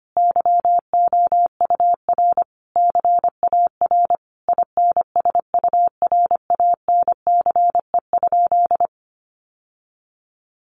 Morse code, heard as a keyed tone, about 25 wpm, 700 Hz. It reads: YOUR CAR INSURANCE?
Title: Sentences from Top 1100 Words 25wpm